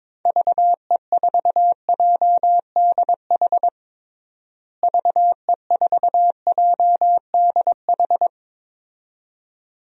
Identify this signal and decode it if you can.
Morse code, heard as a keyed tone, about 21 wpm, 700 Hz. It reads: VE4JDH VE4JDH
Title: DXCC-100 Call Signs - 6 Characters - Repeated 2x - 22wpm